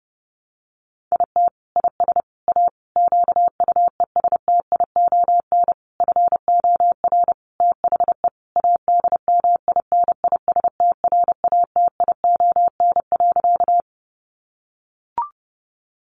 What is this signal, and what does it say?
Morse code, heard as a keyed tone, about 30 wpm, 700 Hz. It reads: IT IS A QUESTION FOR THE ADMINISTRATION.
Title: Sentences from Top 1300 Words 30wpm